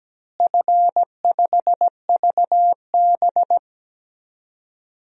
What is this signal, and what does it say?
Morse code, heard as a keyed tone, about 17 wpm, 700 Hz. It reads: F5VB
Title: DXCC-100 Call Signs - 4 Characters - Repeated 3x - 17wpm